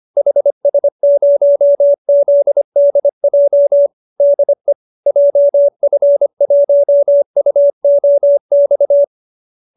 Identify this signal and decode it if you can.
Morse code, heard as a keyed tone, about 25 wpm, 570 Hz. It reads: HS0ZDJ DE JF1UOX